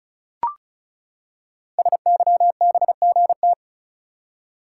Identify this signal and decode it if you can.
Morse code, heard as a keyed tone, about 36 wpm, 690 Hz. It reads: SYBGT
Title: Code Groups - 5 Letters - 35wpm